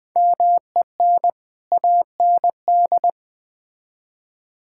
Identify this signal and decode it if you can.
Morse code, heard as a keyed tone, about 20 wpm, 700 Hz. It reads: MEN AND
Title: Build A Phrase - 3-Word Phrases - 20wpm